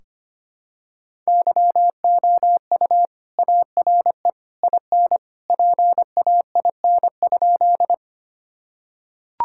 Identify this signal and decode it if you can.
Morse code, heard as a keyed tone, about 25 wpm, 700 Hz. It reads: YOU ARE IN PAIN?
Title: Sentences from Top 1600 Words 25wpm